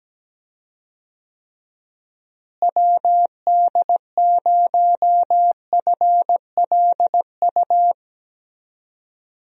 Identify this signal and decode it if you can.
Morse code, heard as a keyed tone, about 17 wpm, 700 Hz. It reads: WD0FLU
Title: USA Call Signs - 6 Characters - Repeated 3x - 17wpm